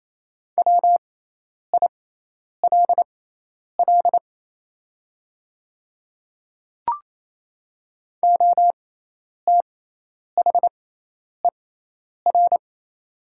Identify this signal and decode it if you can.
Morse code, heard as a keyed tone, about 28 wpm, spaced as for 10 wpm, 700 Hz. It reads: WILL OTHER
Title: 6x Character Spacing - Top 100 Words - 28wpm